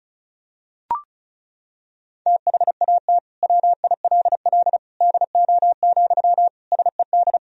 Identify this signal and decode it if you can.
Morse code, heard as a keyed tone, about 34 wpm, 700 Hz. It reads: THAT WILL DO, SED
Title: Sentences from Top 100 Words and Abbreviations 35wpm